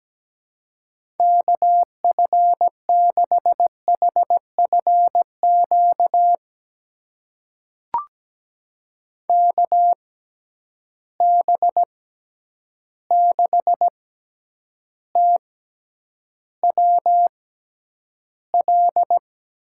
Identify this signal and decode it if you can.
Morse code, heard as a keyed tone, about 17 wpm, spaced as for 9 wpm, 700 Hz. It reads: KF6HFQ KB6TWL
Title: USA Call Signs - 6 Characters - 6x Character Spacing - 17wpm